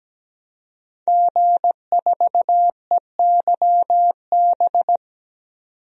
Morse code, heard as a keyed tone, about 17 wpm, 700 Hz. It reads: G4EYB
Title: DXCC-100 Call Signs - 5 Characters - 17wpm